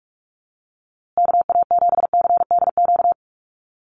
Morse code, heard as a keyed tone, about 44 wpm, 700 Hz. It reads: KA7CDQ
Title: USA Call Signs Repeated 2x 45wpm